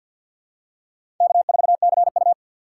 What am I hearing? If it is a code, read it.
Morse code, heard as a keyed tone, about 49 wpm, 700 Hz. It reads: K4CU
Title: USA Call Signs - 4 Characters - 3x Character Spacing - 50wpm